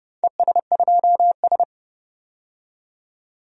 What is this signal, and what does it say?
Morse code, heard as a keyed tone, about 32 wpm, 700 Hz. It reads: ES2S
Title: DXCC-100 Call Signs - 4 Characters - 30wpm